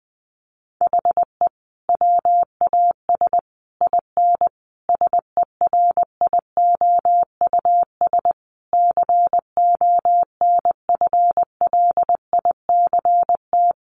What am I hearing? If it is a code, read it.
Morse code, heard as a keyed tone, about 20 wpm, 700 Hz. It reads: HE WAS IN SERIOUS CONFLICT
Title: Sentences from Top 1300 Words 20wpm